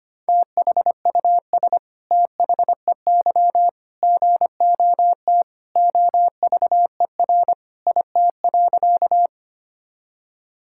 Morse code, heard as a keyed tone, about 25 wpm, 700 Hz. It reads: THUS THEY GOT OVER IT.
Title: Sentences from Top 1000 Words - Review Part 5 - 25wpm